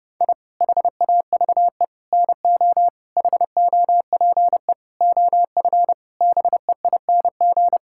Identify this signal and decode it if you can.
Morse code, heard as a keyed tone, about 30 wpm, 700 Hz. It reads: I HAVE NO HOPE OF BEING
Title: Sentences from Top 1000 Words - Review Part 5 - 30wpm